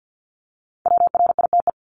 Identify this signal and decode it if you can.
Morse code, heard as a keyed tone, about 50 wpm, 700 Hz. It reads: WRITE
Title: Top 300 Words 50wpm